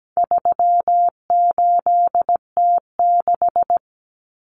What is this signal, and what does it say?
Morse code, heard as a keyed tone, about 17 wpm, 700 Hz. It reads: 38T6
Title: Common Cut Numbers - 4-Digit 17wpm